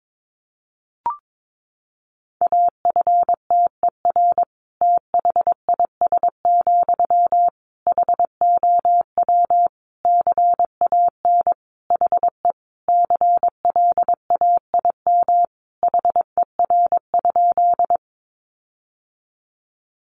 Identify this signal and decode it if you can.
Morse code, heard as a keyed tone, about 22 wpm, 700 Hz. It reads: AFTER THIS, HOW CAN HE CLAIM HER?
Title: Sentences from Top 1500 Words 22wpm